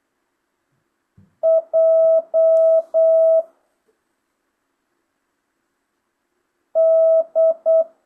Morse code, heard as a keyed tone, about 8 wpm, 650 Hz. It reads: J D
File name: jh8cbh / 2025cw